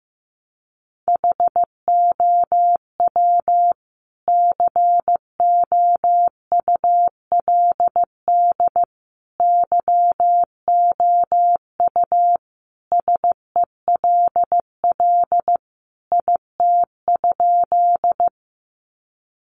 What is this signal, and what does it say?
Morse code, heard as a keyed tone, about 15 wpm, 700 Hz. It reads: HOW COULD YOU SELL IT?
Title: Sentences from Top 1100 Words 15wpm